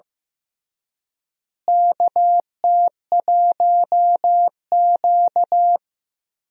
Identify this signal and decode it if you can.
Morse code, heard as a keyed tone, about 15 wpm, 700 Hz. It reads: KT1Q
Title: USA Call Signs - 4 Characters - Repeated 3x - 15wpm